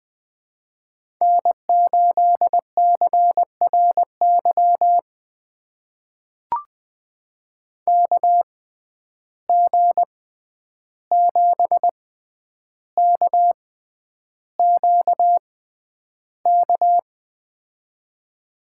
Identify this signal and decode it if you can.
Morse code, heard as a keyed tone, about 20 wpm, spaced as for 10 wpm, 700 Hz. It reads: N8CRY KG7KQK
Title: USA Call Signs - 6x Character Spacing - 20wpm